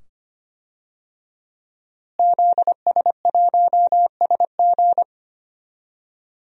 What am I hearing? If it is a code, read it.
Morse code, heard as a keyed tone, about 25 wpm, 700 Hz. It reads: ZS1SG
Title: DXCC-100 Call Signs - 5 Characters - Repeated 2x - 25wpm